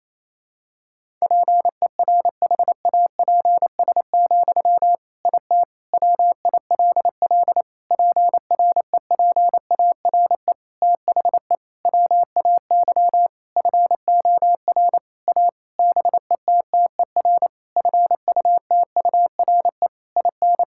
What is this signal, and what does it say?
Morse code, heard as a keyed tone, about 28 wpm, 700 Hz. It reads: PERHAPS, IT WILL PREPARE THE WAY FOR A BETTER FUTURE IN
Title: Sentences from Top 1000 Words - Review Part 6 - 28wpm